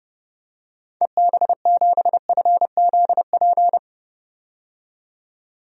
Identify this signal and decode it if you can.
Morse code, heard as a keyed tone, about 30 wpm, 700 Hz. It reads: EB7FZP
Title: DXCC-100 Call Signs - 6 Characters - 30wpm